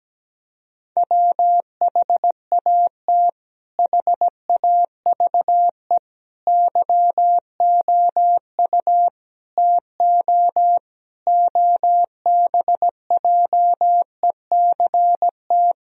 Morse code, heard as a keyed tone, about 17 wpm, 700 Hz. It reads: WHAT HAVE YOU TO OBJECT
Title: Sentences from Top 1000 Words - Review Part 3 - 17wpm